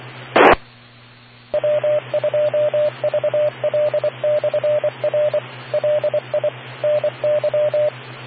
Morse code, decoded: T W2VL/R LI NY